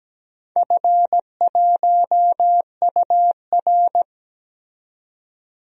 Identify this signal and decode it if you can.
Morse code, heard as a keyed tone, about 17 wpm, 700 Hz. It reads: F1UR
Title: DXCC-100 Call Signs - 4 Characters - 17wpm